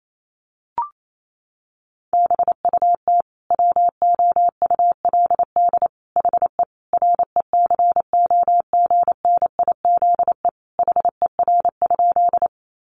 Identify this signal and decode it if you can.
Morse code, heard as a keyed tone, about 28 wpm, 700 Hz. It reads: BUT WOULD HE RECOGNIZE HER?
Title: Sentences from Top 1400 Words 28wpm